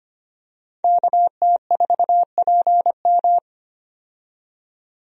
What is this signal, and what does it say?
Morse code, heard as a keyed tone, about 25 wpm, 700 Hz. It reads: KT4PM
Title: USA Call Signs - 5 Characters - 25wpm